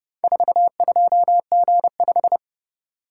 Morse code, heard as a keyed tone, about 30 wpm, 700 Hz. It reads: 42G5